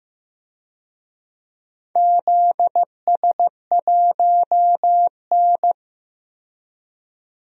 Morse code, heard as a keyed tone, about 15 wpm, 700 Hz. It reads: ZS1N
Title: DXCC-100 Call Signs - 4 Characters - 15wpm